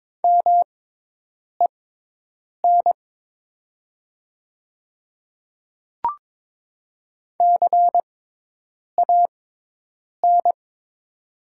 Morse code, heard as a keyed tone, about 22 wpm, spaced as for 8 wpm, 700 Hz. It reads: MEN CAN
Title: 6x Character Spacing - Top 200 Words - 22wpm